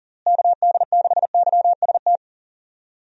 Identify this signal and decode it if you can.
Morse code, heard as a keyed tone, about 40 wpm, 700 Hz. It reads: KD6YST
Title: USA Call Signs - 6 Characters - Repeated 3x - 40wpm